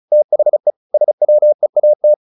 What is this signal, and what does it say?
Morse code, heard as a keyed tone, about 34 wpm, 600 Hz. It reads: THE SWEAT
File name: the old man and the sea 2 35wpm